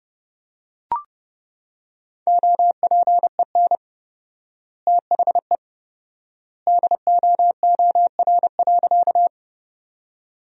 Morse code, heard as a keyed tone, about 30 wpm, 700 Hz. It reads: OPEN THE DOOR.
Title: 4x Word-Spacing - Sentences from Top 300 Words - 30wpm